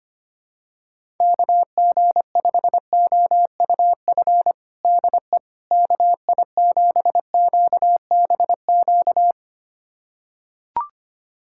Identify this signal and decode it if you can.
Morse code, heard as a keyed tone, about 25 wpm, 700 Hz. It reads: KG5OUF DE KI7QBQ
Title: USA Callsign DE Callsign 25wpm